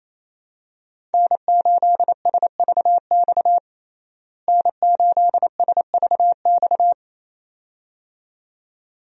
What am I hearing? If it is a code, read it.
Morse code, heard as a keyed tone, about 28 wpm, 700 Hz. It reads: N8SVX N8SVX
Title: USA Call Signs - 5 Characters - Repeated 2x - 28wpm